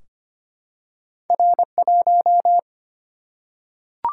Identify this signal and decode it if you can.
Morse code, heard as a keyed tone, about 25 wpm, 700 Hz. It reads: R1
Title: Single-Letter Single-Number 25wpm